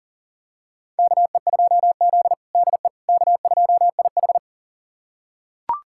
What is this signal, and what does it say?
Morse code, heard as a keyed tone, about 40 wpm, 700 Hz. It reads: KE2Z DE K2IH
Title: USA Callsign DE Callsign - 4 Characters - 40wpm